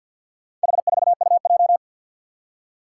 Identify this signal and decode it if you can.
Morse code, heard as a keyed tone, about 46 wpm, 700 Hz. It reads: H4UJ
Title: Four Challenging Letter-Number Combinations 50wpm